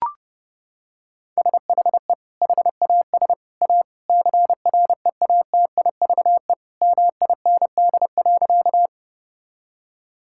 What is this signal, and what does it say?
Morse code, heard as a keyed tone, about 30 wpm, 700 Hz. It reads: SHE HAS A CREATIVE MIND.
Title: Sentences from Top 1000 Words - Review Part 5 - 30wpm